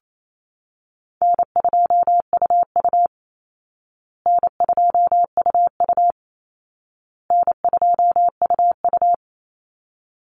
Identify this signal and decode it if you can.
Morse code, heard as a keyed tone, about 28 wpm, 700 Hz. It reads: N2UU N2UU N2UU
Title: Easy Call Signs Repeated 3x - 28wpm